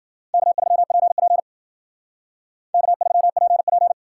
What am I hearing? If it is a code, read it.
Morse code, heard as a keyed tone, about 51 wpm, 700 Hz. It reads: K3PP K3PP